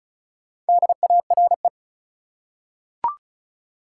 Morse code, heard as a keyed tone, about 36 wpm, 700 Hz. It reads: DARE
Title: Four Letter Words - 35wpm